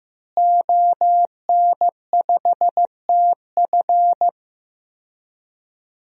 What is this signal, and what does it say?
Morse code, heard as a keyed tone, about 15 wpm, 700 Hz. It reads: ON5TF
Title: DXCC-100 Call Signs - 5 Characters - Repeated 2x - 15wpm